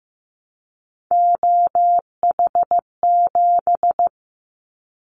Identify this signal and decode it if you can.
Morse code, heard as a keyed tone, about 15 wpm, 700 Hz. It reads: OH7
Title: Callsign Prefixes for Top 25 DXCC - Repeated 3x - 15wpm